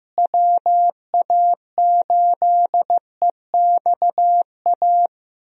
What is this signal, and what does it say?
Morse code, heard as a keyed tone, about 15 wpm, 700 Hz. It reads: WA8EXA
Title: USA Call Signs - 6 Characters - Repeated 2x - 15wpm